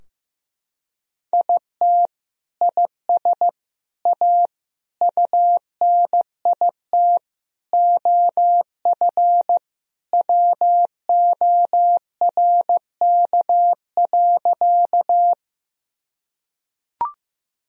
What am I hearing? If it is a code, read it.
Morse code, heard as a keyed tone, about 15 wpm, 700 Hz. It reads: IT IS A UNIT OF WORK.
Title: Sentences from Top 1000 Words - Review Part 2 - 15wpm